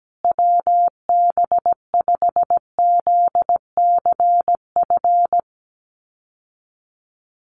Morse code, heard as a keyed tone, about 17 wpm, 700 Hz. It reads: WB5ZCF
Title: USA Call Signs Repeated 3x 17wpm